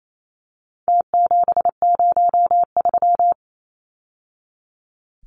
Morse code, heard as a keyed tone, about 28 wpm, 700 Hz. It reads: T703
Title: Numbers - Contest Style - 4-Digit 28wpm